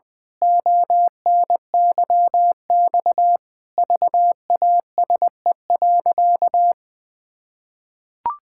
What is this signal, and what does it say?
Morse code, heard as a keyed tone, about 20 wpm, 700 Hz. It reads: ONYX VASE.
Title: Pangram Challenge v2 - 20wpm